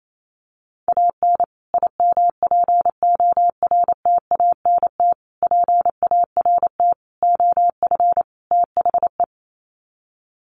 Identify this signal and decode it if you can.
Morse code, heard as a keyed tone, about 28 wpm, 700 Hz. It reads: AN IMPORTANT PART OF THE
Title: Sets of 5 Words Encore 28wpm